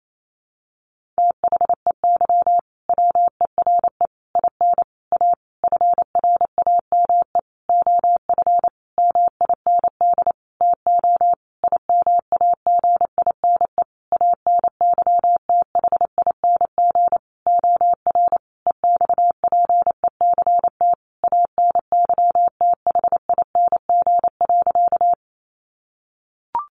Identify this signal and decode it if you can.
Morse code, heard as a keyed tone, about 28 wpm, 700 Hz. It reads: THEY WERE IN A FRAME OF MIND TO IMAGINE ANYTHING OR EXPECT ANYTHING.
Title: Sentences from Top 1400 Words 28wpm